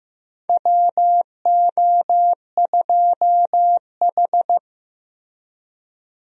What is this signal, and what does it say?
Morse code, heard as a keyed tone, about 15 wpm, 700 Hz. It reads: WO2H
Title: USA Call Signs - 4 Characters - Repeated 3x - 15wpm